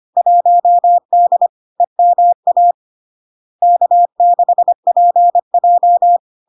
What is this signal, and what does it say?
Morse code, heard as a keyed tone, about 25 wpm, 700 Hz. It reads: 1D EMA K6PJ
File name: Log-in-CW-25wpm-700-Hz